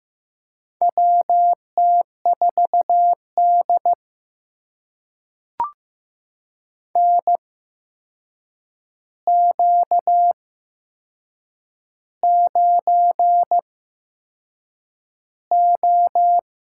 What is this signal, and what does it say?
Morse code, heard as a keyed tone, about 15 wpm, spaced as for 6 wpm, 700 Hz. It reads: WT4D NQ9O
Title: USA Call Signs - 4 Characters - 8x Character Spacing - 15wpm